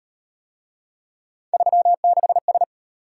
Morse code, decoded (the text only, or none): 36S